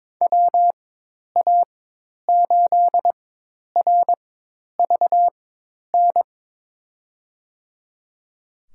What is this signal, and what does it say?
Morse code, heard as a keyed tone, about 21 wpm, spaced as for 10 wpm, 700 Hz. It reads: WA8RVN
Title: USA Call Signs - 6 Characters - 4x Character Spacing - 22wpm